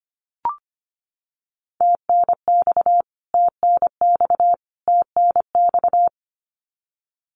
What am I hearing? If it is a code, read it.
Morse code, heard as a keyed tone, about 25 wpm, 700 Hz. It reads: TNX TNX TNX
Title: Abbreviations Repeated 3x - 25wpm